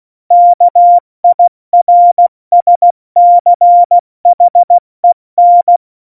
Morse code, heard as a keyed tone, about 16 wpm, 700 Hz. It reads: KIRSCHEN
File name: Morsekurs / Witze / Witz 14. Januar 2020, 70 F80